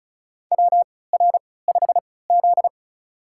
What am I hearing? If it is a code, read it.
Morse code, heard as a keyed tone, about 34 wpm, spaced as for 20 wpm, 700 Hz. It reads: WR5Z
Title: USA Call Signs - 4 Characters - 3x Character Spacing - 35wpm